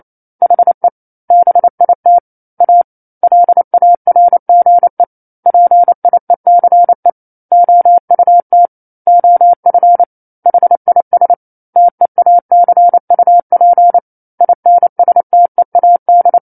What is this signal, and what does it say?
Morse code, heard as a keyed tone, about 28 wpm, 700 Hz. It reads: HE BIT A LARGE PIECE OUT OF HIS TEACUP INSTEAD